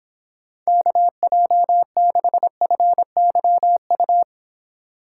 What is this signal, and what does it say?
Morse code, heard as a keyed tone, about 26 wpm, 700 Hz. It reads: KJ6FYU